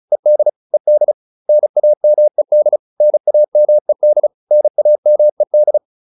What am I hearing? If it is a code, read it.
Morse code, heard as a keyed tone, about 34 wpm, 600 Hz. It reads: ED ED NAMED NAMED NAMED